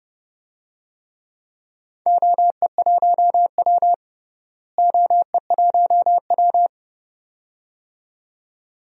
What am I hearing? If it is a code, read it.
Morse code, heard as a keyed tone, about 30 wpm, 700 Hz. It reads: OE1W OE1W
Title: DXCC-100 Call Signs - 4 Characters - Repeated 2x - 30wpm